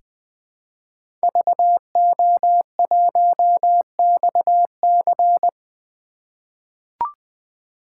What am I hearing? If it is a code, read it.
Morse code, heard as a keyed tone, about 20 wpm, 700 Hz. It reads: VO1XC
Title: K1USN SST QSO Practice - 20wpm